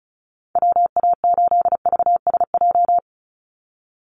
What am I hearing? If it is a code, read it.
Morse code, heard as a keyed tone, about 34 wpm, 700 Hz. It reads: WA8VSJ
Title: Call Signs - 35wpm